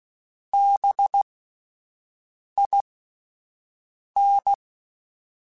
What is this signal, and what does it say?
Morse code, heard as a keyed tone, about 16 wpm, spaced as for 5 wpm, 800 Hz. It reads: BIN